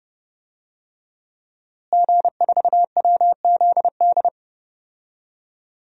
Morse code, heard as a keyed tone, about 30 wpm, 700 Hz. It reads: G4WZD
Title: DXCC-100 Call Signs - 5 Characters - Repeated 2x - 30wpm